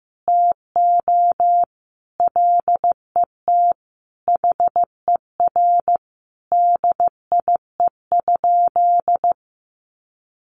Morse code, heard as a keyed tone, about 15 wpm, 700 Hz. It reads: TO LET HER DIE?
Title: Sentences from Top 1300 Words 15wpm